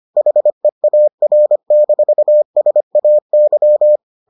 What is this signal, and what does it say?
Morse code, heard as a keyed tone, about 25 wpm, 600 Hz. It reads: HEAR-SAY